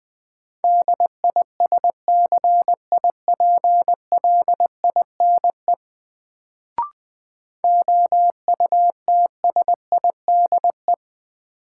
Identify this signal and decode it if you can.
Morse code, heard as a keyed tone, about 20 wpm, 700 Hz. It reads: DISCIPLINE OUTSIDE
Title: Sentences from Top 900 Words - New Word Drill - 20wpm